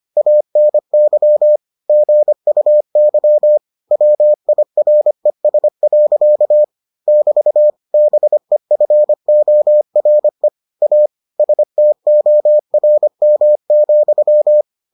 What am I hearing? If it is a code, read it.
Morse code, heard as a keyed tone, about 25 wpm, 600 Hz. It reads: ANY GUY WIRES. <BT> BEFORE A STORM,